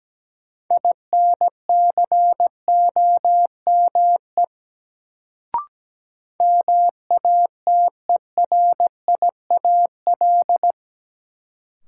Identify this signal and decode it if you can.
Morse code, heard as a keyed tone, about 17 wpm, 700 Hz. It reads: INCOME MATERIAL
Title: Sentences from Top 1000 Words - Review Part 5 - New Word Drill - 17wpm